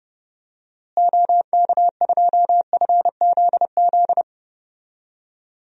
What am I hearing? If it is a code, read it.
Morse code, heard as a keyed tone, about 30 wpm, 700 Hz. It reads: OK2FZZ
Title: DXCC-100 Call Signs - 6 Characters - Repeated 3x - 30wpm